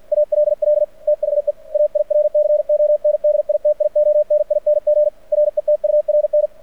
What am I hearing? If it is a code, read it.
Morse code, heard as a keyed tone, about 50 wpm, 600 Hz. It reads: ALL THE AIRCONDITIONING RETURN
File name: archived / CRSnet